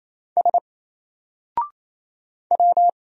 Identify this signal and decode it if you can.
Morse code, heard as a keyed tone, about 27 wpm, 700 Hz. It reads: S W
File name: Single-Letter-Number-28wpm